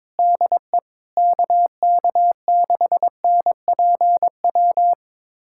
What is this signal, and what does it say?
Morse code, heard as a keyed tone, about 22 wpm, 700 Hz. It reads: DE KK6NPW
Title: USA Callsign DE Callsign - 6 Characters - 22wpm